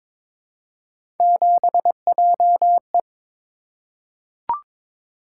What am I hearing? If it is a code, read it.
Morse code, heard as a keyed tone, about 22 wpm, 700 Hz. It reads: 7JE
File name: practice / 22wpm / Three Letter-Number Combinations - 22wpm